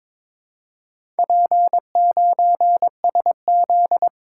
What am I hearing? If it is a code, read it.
Morse code, heard as a keyed tone, about 22 wpm, 700 Hz. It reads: P9SZ
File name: practice / 22wpm / Four Letter-Number Combinations - 22wpm